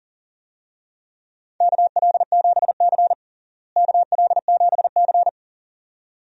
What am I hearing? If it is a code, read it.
Morse code, heard as a keyed tone, about 37 wpm, 700 Hz. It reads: KL7C KL7C
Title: DXCC-100 Call Signs - 4 Characters - Repeated 2x - 40wpm